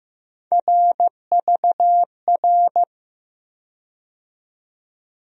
Morse code, heard as a keyed tone, about 15 wpm, 700 Hz. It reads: RVR
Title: USA Call Sign Postfixes - 3 Letters 15wpm